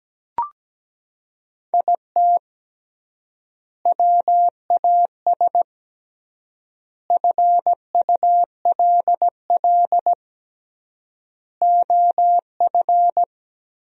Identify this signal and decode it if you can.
Morse code, heard as a keyed tone, about 17 wpm, 700 Hz. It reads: IT WAS FULL OF
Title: 3x Word-Spacing - Sentences from Top 500 Words - 17wpm